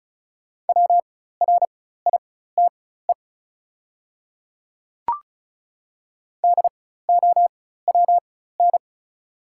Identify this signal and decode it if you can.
Morse code, heard as a keyed tone, about 34 wpm, spaced as for 16 wpm, 700 Hz. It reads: WRITE DOWN
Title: 4x Character Spacing - Top 100 Words - 35wpm